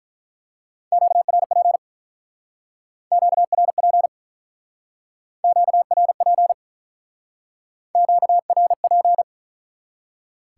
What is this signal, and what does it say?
Morse code, heard as a keyed tone, about 44 wpm, 700 Hz. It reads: QRP QRP QRP QRP